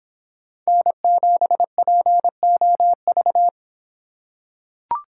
Morse code, heard as a keyed tone, about 25 wpm, 700 Hz. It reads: N7POV